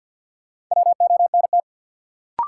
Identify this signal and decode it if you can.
Morse code, heard as a keyed tone, about 46 wpm, 700 Hz. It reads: WONT
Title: Sentences from Top 1000 Words - Review Part 6 - New Word Drill - 50wpm